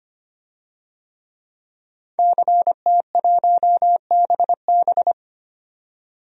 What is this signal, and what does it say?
Morse code, heard as a keyed tone, about 25 wpm, 700 Hz. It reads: CT1BB
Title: DXCC-100 Call Signs - 5 Characters - 25wpm